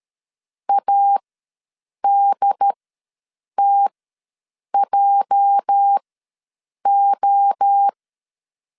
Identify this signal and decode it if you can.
Morse code, heard as a keyed tone, about 13 wpm, spaced as for 7 wpm, 800 Hz. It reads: ADTJO
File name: LEZIONI / 8 YJ45 / 8 YJ45 ws 12wpm 3146